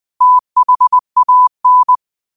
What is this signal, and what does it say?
Morse code, heard as a keyed tone, about 20 wpm, 1000 Hz. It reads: THAN
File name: qotdfiles / 20wpm